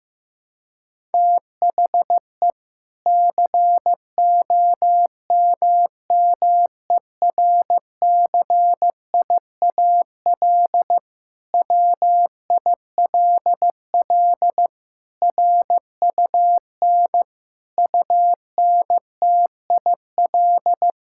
Morse code, heard as a keyed tone, about 15 wpm, 700 Hz. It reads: THE COMMERCIAL WILL RUN UNTIL
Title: Sentences from Top 1000 Words - Review Part 5 - 15wpm